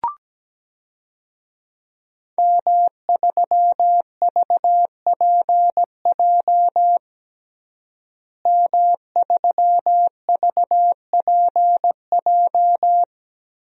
Morse code, heard as a keyed tone, about 17 wpm, 700 Hz. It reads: M3VPJ M3VPJ
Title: DXCC-100 Call Signs - 5 Characters - Repeated 2x - 17wpm